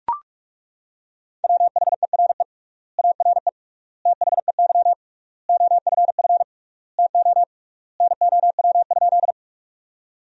Morse code, heard as a keyed tone, about 41 wpm, 700 Hz. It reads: WHERE ARE THEY OFF TO NOW?
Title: 3x Word-Spacing - Sentences from Top 200 Words - 45wpm